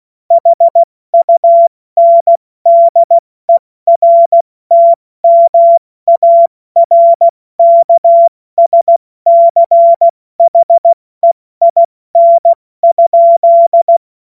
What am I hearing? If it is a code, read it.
Morse code, heard as a keyed tone, about 16 wpm, 680 Hz. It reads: HUNDERTMARKSCHEIN?
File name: Witz-21-Oktober-2025-70-F80